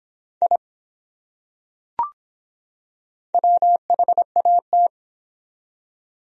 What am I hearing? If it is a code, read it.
Morse code, heard as a keyed tone, about 23 wpm, 700 Hz. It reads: I WHAT